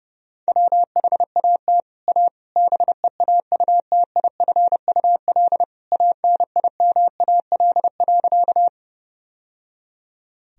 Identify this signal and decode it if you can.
Morse code, heard as a keyed tone, about 31 wpm, 700 Hz. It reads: WHAT A BEAUTIFUL ANIMAL.
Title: Sentences from Top 1000 Words - Review Part 1 - 30wpm